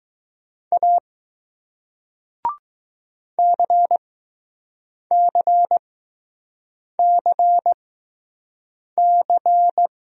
Morse code, heard as a keyed tone, about 21 wpm, spaced as for 6 wpm, 700 Hz. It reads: A CCCC